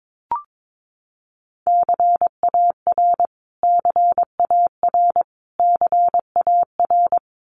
Morse code, heard as a keyed tone, about 22 wpm, 700 Hz. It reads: CAR CAR CAR
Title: Top 500 Words Repeated 22wpm